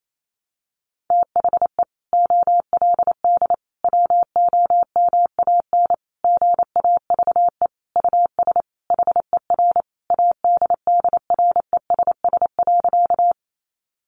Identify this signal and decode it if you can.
Morse code, heard as a keyed tone, about 28 wpm, 700 Hz. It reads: THE OLD WOMAN GAVE US HER ADDRESS.
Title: Sentences from Top 1100 Words 28wpm